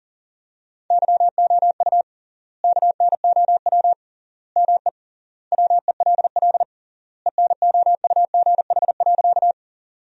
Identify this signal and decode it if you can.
Morse code, heard as a keyed tone, about 40 wpm, 700 Hz. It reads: YOU KNOW ME WELL ENOUGH.
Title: 3x Word-Spacing - Sentences from Top 400 Words - 40wpm